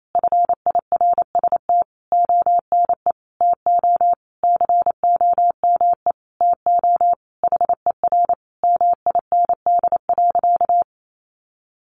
Morse code, heard as a keyed tone, about 28 wpm, 700 Hz. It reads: FIRST ONE TO COME TO HER MIND.
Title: Sentences from Top 1300 Words 28wpm